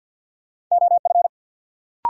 Morse code, heard as a keyed tone, about 52 wpm, 700 Hz. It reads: OF
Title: Sets of 2 Words - New Word Drill - 50wpm